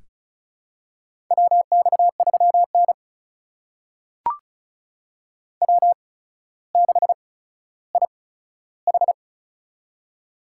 Morse code, heard as a keyed tone, about 36 wpm, spaced as for 15 wpm, 700 Hz. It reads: WX3N W6IH
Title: USA Call Signs - 4 Characters - 8x Character Spacing - 35wpm